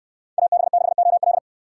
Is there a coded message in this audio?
Yes: Morse code from 0.4 to 1.4 s, heard as a keyed tone, about 69 wpm, 700 Hz. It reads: AB6OL